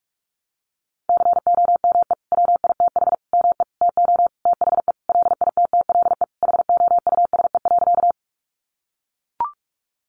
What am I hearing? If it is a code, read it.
Morse code, heard as a keyed tone, about 44 wpm, 700 Hz. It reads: COME WITH ME TO THE LITTLE HOUSE.
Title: Sentences from Top 200 Words 45wpm